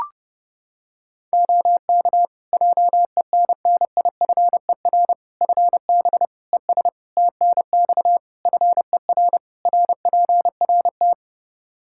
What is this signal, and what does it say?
Morse code, heard as a keyed tone, about 30 wpm, 700 Hz. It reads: OK JENNIFER FB ES TNX FER RPRT
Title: Practice QSO - Exchange 2 - Info - 30wpm